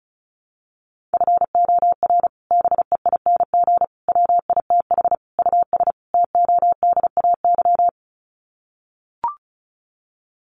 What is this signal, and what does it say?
Morse code, heard as a keyed tone, about 34 wpm, 700 Hz. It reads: FOR BEING WITH US TODAY
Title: Sets of 5 Words Encore2 35wpm